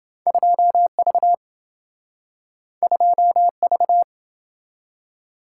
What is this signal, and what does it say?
Morse code, heard as a keyed tone, about 26 wpm, 700 Hz. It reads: 2V 2V